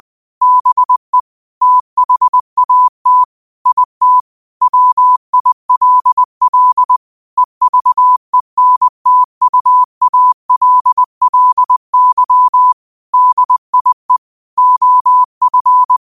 Morse code, heard as a keyed tone, about 20 wpm, 1000 Hz. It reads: BE THAT IT WILL EVENTUALLY DIE OF
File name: qotdfiles / 20wpm